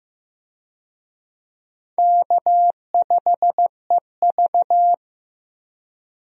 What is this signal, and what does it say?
Morse code, heard as a keyed tone, about 15 wpm, 700 Hz. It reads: K5EV
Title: USA Call Signs - 4 Characters - 15wpm